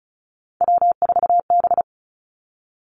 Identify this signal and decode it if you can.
Morse code, heard as a keyed tone, about 36 wpm, 700 Hz. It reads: W4B
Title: Three Challenging Letter-Number Combinations 35wpm